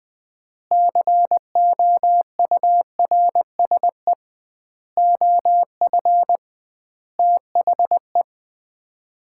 Morse code, heard as a keyed tone, about 20 wpm, 700 Hz. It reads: COURSE OF THE
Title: 2x Word-Spacing - Sets of 5 Words - 20wpm